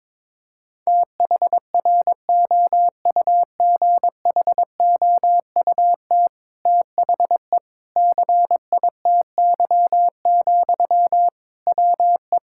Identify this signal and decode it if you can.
Morse code, heard as a keyed tone, about 22 wpm, 700 Hz. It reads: THROUGHOUT THE CITY, WE
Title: Sentences from Top 1000 Words - Review Part 5 - 22wpm